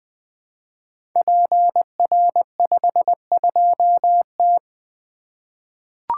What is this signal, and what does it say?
Morse code, heard as a keyed tone, about 20 wpm, 700 Hz. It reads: PR52T
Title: Code Groups - 5 Letter:Numbers - 20wpm